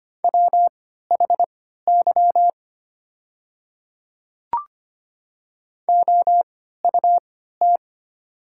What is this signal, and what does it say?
Morse code, heard as a keyed tone, about 25 wpm, spaced as for 14 wpm, 700 Hz. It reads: WHY OUT
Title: 3x Character Spacing - Top 300 Words - 25wpm